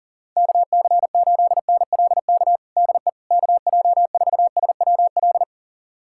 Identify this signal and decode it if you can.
Morse code, heard as a keyed tone, about 37 wpm, 700 Hz. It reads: KC8NLK DE KJ4SWL
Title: USA Callsign DE Callsign 40wpm